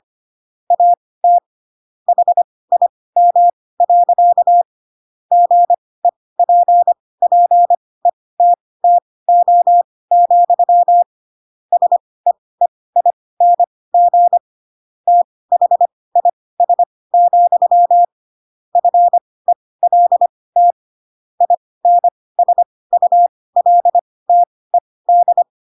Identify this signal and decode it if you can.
Morse code, heard as a keyed tone, about 25 wpm, spaced as for 18 wpm, 700 Hz. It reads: AT HIM. GEPPETTO, SEEING THIS, FELT INSULTED
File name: Pinocchio ch3 12 25 ik1hge